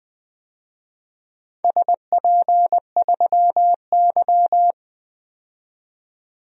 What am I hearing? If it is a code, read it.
Morse code, heard as a keyed tone, about 20 wpm, 700 Hz. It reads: SP3Y
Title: DXCC-100 Call Signs - 4 Characters - 20wpm